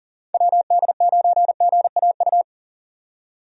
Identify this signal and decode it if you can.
Morse code, heard as a keyed tone, about 40 wpm, 700 Hz. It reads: WD9GAU
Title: USA Call Signs - 6 Characters - 40wpm